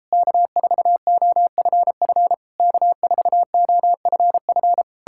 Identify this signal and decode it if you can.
Morse code, heard as a keyed tone, about 37 wpm, 700 Hz. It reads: K4OFF K4OFF